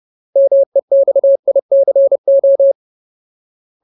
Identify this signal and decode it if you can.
Morse code, heard as a keyed tone, about 30 wpm, 550 Hz. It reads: MEXICO
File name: Ultimate Word List / 30WPM Ultimate